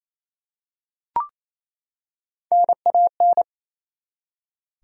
Morse code, heard as a keyed tone, about 28 wpm, 700 Hz. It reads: NAN
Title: USA Common Names - Max 5 Letters - 28wpm